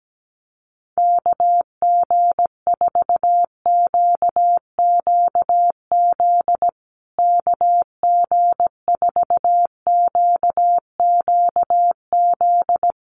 Morse code, heard as a keyed tone, about 17 wpm, 700 Hz. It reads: KG4QQZ KG4QQZ
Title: USA Call Signs Repeated 2x 17wpm